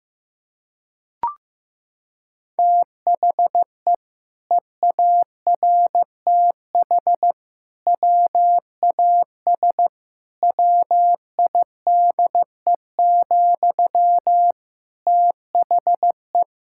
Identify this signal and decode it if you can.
Morse code, heard as a keyed tone, about 16 wpm, 700 Hz. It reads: THE EARTH WAS WIDE, THE
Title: Sentences from Top 1000 Words - Review Part 6 - 15wpm